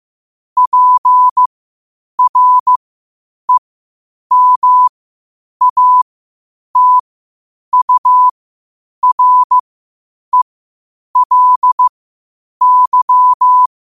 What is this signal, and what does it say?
Morse code, heard as a keyed tone, about 15 wpm, spaced as for 8 wpm, 1000 Hz. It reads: PREMATURELY
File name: qotdfiles / 8wpm